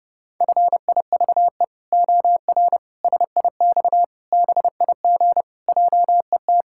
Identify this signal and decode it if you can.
Morse code, heard as a keyed tone, about 32 wpm, 700 Hz. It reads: FIVE OR SIX BIG JET
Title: Pangram Challenge v2 - 30wpm